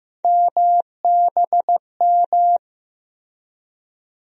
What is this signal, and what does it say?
Morse code, heard as a keyed tone, about 15 wpm, 700 Hz. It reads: MBM
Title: USA Call Sign Postfixes - 3 Letters 15wpm